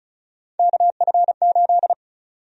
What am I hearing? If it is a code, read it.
Morse code, heard as a keyed tone, about 34 wpm, 700 Hz. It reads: KF8
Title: USA Call Sign Prefixes - 35wpm